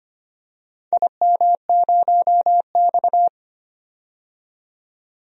Morse code, heard as a keyed tone, about 25 wpm, 700 Hz. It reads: IM0X
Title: DXCC-100 Call Signs - 4 Characters - 25wpm